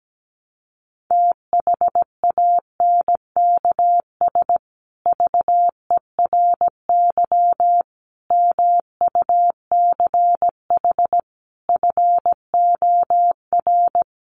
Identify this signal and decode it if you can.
Morse code, heard as a keyed tone, about 17 wpm, 700 Hz. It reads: THANKS VERY MUCH FOR
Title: Sets of 5 Words Encore2 17wpm